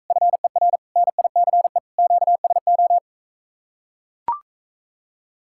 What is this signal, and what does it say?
Morse code, heard as a keyed tone, about 44 wpm, 700 Hz. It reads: FER NICE QSO